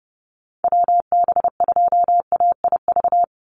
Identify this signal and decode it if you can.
Morse code, heard as a keyed tone, about 30 wpm, 700 Hz. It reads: WB2AIV
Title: Call Signs Repeated 2x - 30wpm